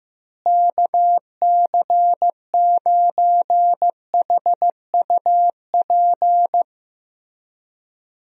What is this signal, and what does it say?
Morse code, heard as a keyed tone, about 15 wpm, 700 Hz. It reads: KC9HUP
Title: USA Call Signs - 6 Characters - Repeated 3x - 15wpm